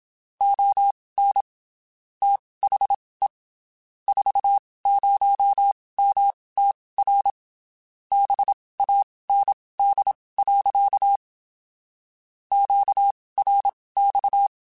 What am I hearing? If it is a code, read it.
Morse code, heard as a keyed tone, about 27 wpm, spaced as for 19 wpm, 800 Hz. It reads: ON THE 40MTR BAND. QRX
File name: (spaced) / QSOs Spaced-mp3s / WA7QOX 2 @25 sp